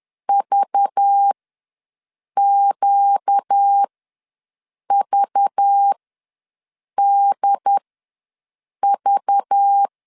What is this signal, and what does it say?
Morse code, heard as a keyed tone, about 10 wpm, spaced as for 6 wpm, 800 Hz. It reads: VQVDV
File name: LEZIONI / 5 LBVC / 5 LBV ws 10wpm 3645